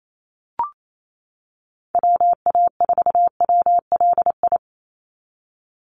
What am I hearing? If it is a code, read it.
Morse code, heard as a keyed tone, about 28 wpm, 700 Hz. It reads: WA4WLI
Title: Call Signs - 28wpm